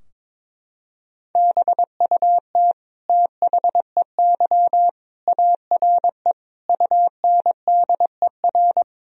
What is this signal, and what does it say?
Morse code, heard as a keyed tone, about 22 wpm, 700 Hz. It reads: BUT THEY ARE UNDER
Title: Sentences from Top 1000 Words - Review Part 4 - 22wpm